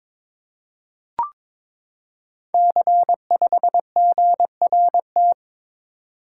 Morse code, heard as a keyed tone, about 23 wpm, 690 Hz. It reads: C5GRT
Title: Code Groups - 5 Letter:Numbers - 22wpm